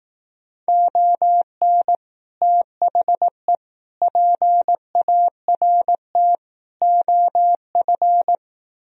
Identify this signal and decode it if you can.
Morse code, heard as a keyed tone, about 18 wpm, 700 Hz. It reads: ON THE PART OF
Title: Speed-Racing - Sets of 4 Words 15wpm